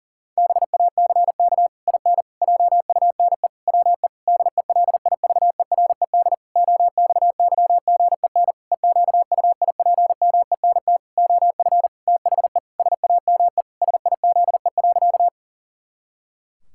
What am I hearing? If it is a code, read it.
Morse code, heard as a keyed tone, about 37 wpm, 700 Hz. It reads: BACK IN JUNE WE DELIVERED OXYGEN EQUIPMENT OF THE SAME SIZE.
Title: Pangram Challenge v2 - 40wpm